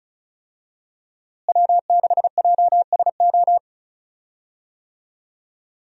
Morse code, heard as a keyed tone, about 36 wpm, 700 Hz. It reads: W6JSO
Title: USA Call Signs - 5 Characters - Repeated 3x - 35wpm